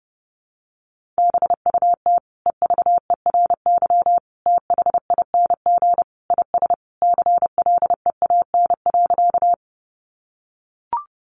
Morse code, heard as a keyed tone, about 30 wpm, 700 Hz. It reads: BUT EVERYTHING IS CLEAN.
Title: Sentences from Top 1100 Words 30wpm